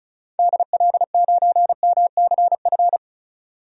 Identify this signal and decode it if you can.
Morse code, heard as a keyed tone, about 36 wpm, 700 Hz. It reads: DL9MCF
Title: DXCC-100 Call Signs - 6 Characters - Repeated 3x - 35wpm